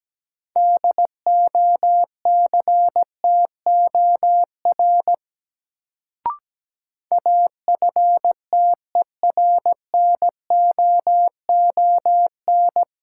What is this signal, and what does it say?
Morse code, heard as a keyed tone, about 17 wpm, 700 Hz. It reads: DOCTOR AFTERNOON
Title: Sentences from Top 1500 Words - New Word Drill - 17wpm